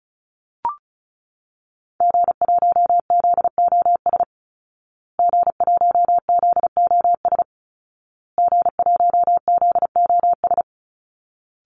Morse code, heard as a keyed tone, about 34 wpm, 700 Hz. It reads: G1ZOS G1ZOS G1ZOS
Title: Call Signs Repeated 2x - 35wpm